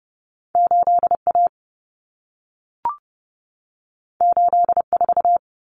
Morse code, heard as a keyed tone, about 30 wpm, 700 Hz. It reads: 8A 84
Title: Common Cut Numbers - 2-Digit 30wpm